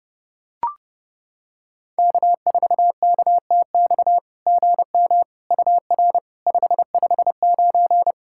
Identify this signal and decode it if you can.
Morse code, heard as a keyed tone, about 30 wpm, 690 Hz. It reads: K4KTX GM UR 559
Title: SOTA QSO Practice 30wpm